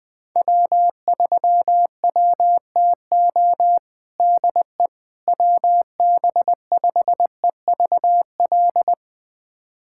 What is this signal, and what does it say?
Morse code, heard as a keyed tone, about 20 wpm, 700 Hz. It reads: W3WTO DE WB5EVL
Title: USA Callsign DE Callsign 20wpm